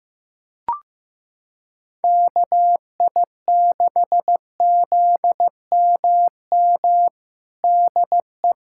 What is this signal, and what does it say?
Morse code, heard as a keyed tone, about 15 wpm, 700 Hz. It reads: KI6ZMM DE
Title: USA Callsign DE Callsign 15wpm